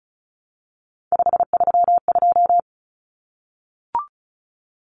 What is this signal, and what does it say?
Morse code, heard as a keyed tone, about 34 wpm, 700 Hz. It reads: 532
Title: Numbers - 3-Digit 35wpm